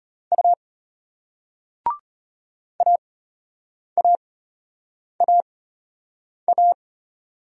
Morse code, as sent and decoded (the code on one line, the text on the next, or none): ..- / .- .- .- .-
U AAAA